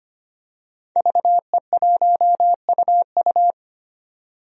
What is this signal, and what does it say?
Morse code, heard as a keyed tone, about 25 wpm, 700 Hz. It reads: VE1UU
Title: DXCC-100 Call Signs - 5 Characters - 25wpm